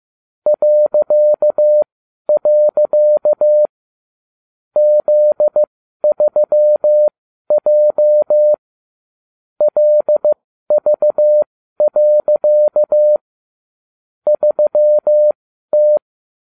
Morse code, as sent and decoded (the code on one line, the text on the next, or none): .-.-.- .-.-.- / --.. ...-- .--- / .-.. ...- .-.-.- / ...-- -
.. Z3J LV. 3T